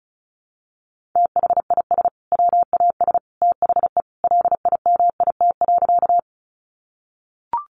Morse code, heard as a keyed tone, about 34 wpm, 700 Hz. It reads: THIS WAS THE LIMIT.
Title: Sentences from Top 1500 Words 35wpm